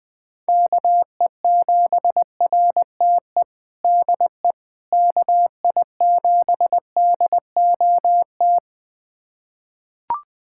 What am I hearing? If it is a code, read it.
Morse code, heard as a keyed tone, about 21 wpm, 700 Hz. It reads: KE7RTE DE KI7DOT
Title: USA Callsign DE Callsign - 6 Characters - 20wpm